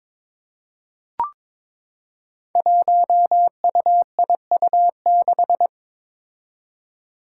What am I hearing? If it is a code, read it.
Morse code, heard as a keyed tone, about 21 wpm, 700 Hz. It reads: 1UIU6
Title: Five Challenging Letter-Number Combinations 22wpm